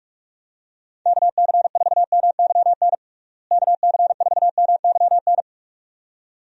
Morse code, heard as a keyed tone, about 46 wpm, 700 Hz. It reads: KC4MYN KC4MYN
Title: USA Call Signs - 6 Characters - Repeated 2x - 45wpm